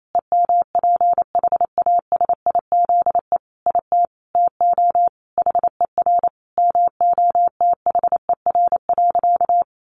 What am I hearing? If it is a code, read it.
Morse code, heard as a keyed tone, about 28 wpm, 700 Hz. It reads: EMPHASIZE IT TO HER MOTHER.
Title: Sentences from Top 1500 Words 28wpm